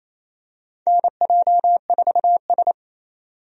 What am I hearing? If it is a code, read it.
Morse code, heard as a keyed tone, about 27 wpm, 700 Hz. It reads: NJ4S
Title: USA Call Signs - 4 Characters - 28wpm